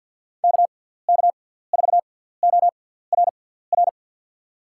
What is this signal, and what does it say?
Morse code, heard as a keyed tone, about 47 wpm, spaced as for 17 wpm, 700 Hz. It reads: KK4ORR
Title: USA Call Signs - 6 Characters - 6x Character Spacing - 50wpm